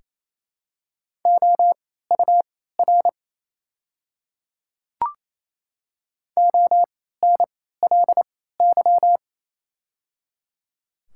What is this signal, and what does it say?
Morse code, heard as a keyed tone, about 28 wpm, spaced as for 16 wpm, 700 Hz. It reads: OUR ONLY
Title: 3x Character Spacing - Top 200 Words - 28wpm